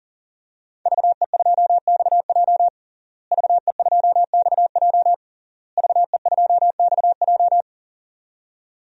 Morse code, heard as a keyed tone, about 40 wpm, 700 Hz. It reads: VE2XJ VE2XJ VE2XJ
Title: DXCC-100 Call Signs - 5 Characters - Repeated 3x - 40wpm